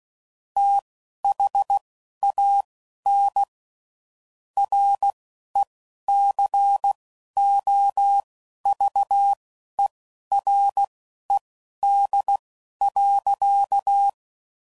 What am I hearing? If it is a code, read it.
Morse code, heard as a keyed tone, about 16 wpm, spaced as for 11 wpm, 800 Hz. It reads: THAN RECOVERED.
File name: Sayings / Quotes / Character